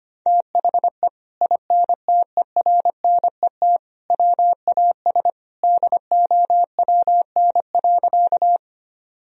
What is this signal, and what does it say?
Morse code, heard as a keyed tone, about 25 wpm, 700 Hz. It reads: THE INTERNET WAS DOWN.
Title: Sentences from Top 1000 Words - Review Part 4 - 25wpm